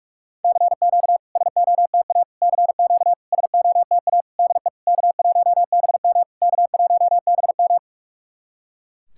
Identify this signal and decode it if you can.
Morse code, heard as a keyed tone, about 44 wpm, 690 Hz. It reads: CQ SOTA CQ SOTA DE K1BM K1BM
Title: SOTA QSO Practice 45wpm